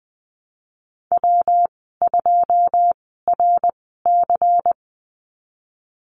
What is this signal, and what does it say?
Morse code, heard as a keyed tone, about 20 wpm, spaced as for 15 wpm, 700 Hz. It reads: W2RC
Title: 122 Lesson 20g - Number 3 - Call Signs 20wpm